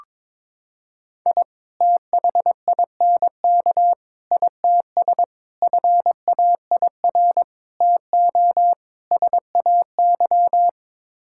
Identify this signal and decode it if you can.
Morse code, heard as a keyed tone, about 22 wpm, 700 Hz. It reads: I THINK ITS FAIR TO SAY
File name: practice / 22wpm / Sets of 6 Words - 22wpm